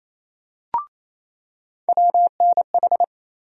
Morse code, heard as a keyed tone, about 28 wpm, 700 Hz. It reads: WNH
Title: USA Call Sign Postfixes - 3 Letters 28wpm